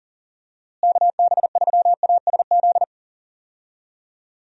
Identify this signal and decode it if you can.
Morse code, heard as a keyed tone, about 40 wpm, 700 Hz. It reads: KB3ASZ
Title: USA Call Signs - 6 Characters - Repeated 2x - 40wpm